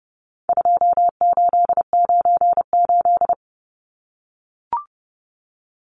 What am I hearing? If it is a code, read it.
Morse code, heard as a keyed tone, about 30 wpm, 700 Hz. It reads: 2898
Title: Numbers - 4-Digit 30wpm